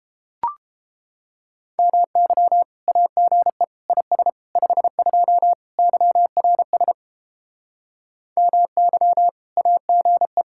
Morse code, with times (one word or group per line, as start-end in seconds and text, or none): 1.8-2.6 MY
2.9-3.6 AGE
3.9-4.3 IS
4.6-5.5 52
5.8-6.9 YRS
8.4-9.3 MY
9.6-10.4 AGE